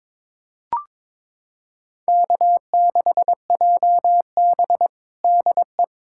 Morse code, heard as a keyed tone, about 22 wpm, 700 Hz. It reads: K6JB DE
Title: USA Callsign DE Callsign - 4 Characters - 22wpm